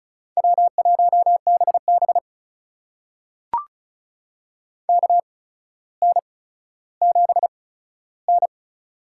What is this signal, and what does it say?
Morse code, heard as a keyed tone, about 36 wpm, spaced as for 15 wpm, 700 Hz. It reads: W1BB KN7N
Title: USA Call Signs - 4 Characters - 8x Character Spacing - 35wpm